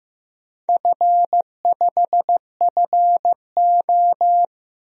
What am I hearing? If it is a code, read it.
Morse code, heard as a keyed tone, about 15 wpm, 700 Hz. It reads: F5FO
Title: DXCC-100 Call Signs - 4 Characters - 15wpm